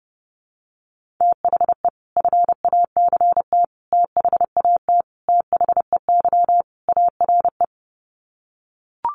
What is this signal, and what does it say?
Morse code, heard as a keyed tone, about 30 wpm, 700 Hz. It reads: THE FACT THAT THEY ARE
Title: Sets of 5 Words Encore2 30wpm